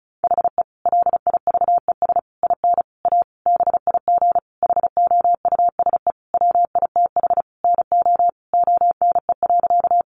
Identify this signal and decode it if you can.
Morse code, heard as a keyed tone, about 34 wpm, 700 Hz. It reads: HE LIVES IN A BIG HOUSE WITH NO ONE.
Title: Additional Easy Phrases 35wpm